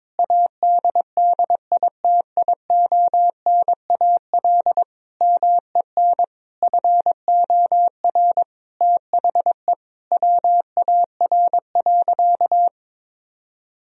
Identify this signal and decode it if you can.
Morse code, heard as a keyed tone, about 22 wpm, 700 Hz. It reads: ADDITIONAL MEN FOR THE WAR.
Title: Sentences from Top 1000 Words - Review Part 4 - 22wpm